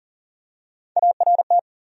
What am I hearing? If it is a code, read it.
Morse code, heard as a keyed tone, about 40 wpm, 700 Hz. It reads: ART
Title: USA Common Names - Max 3 Letters - 40wpm